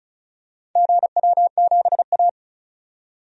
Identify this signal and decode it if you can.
Morse code, heard as a keyed tone, about 36 wpm, 700 Hz. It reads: GW7A
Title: DXCC-100 Call Signs - 4 Characters - 35wpm